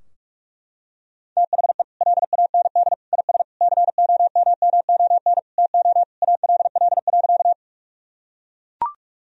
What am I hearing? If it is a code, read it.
Morse code, heard as a keyed tone, about 41 wpm, 700 Hz. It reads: THE LAND IS COMMON TO ALL.
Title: Sentences from Top 1000 Words - Review Part 3 - 45wpm